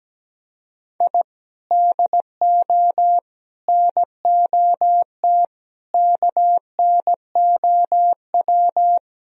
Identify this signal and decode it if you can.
Morse code, heard as a keyed tone, about 17 wpm, 700 Hz. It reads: I DO NOT KNOW
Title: Speed-Racing - Sets of 5 Words 17wpm